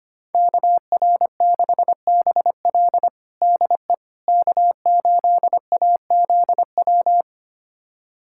Morse code, heard as a keyed tone, about 25 wpm, 700 Hz. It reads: KR6BL DE K8AZW
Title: USA Callsign DE Callsign - 5 Characters - 25wpm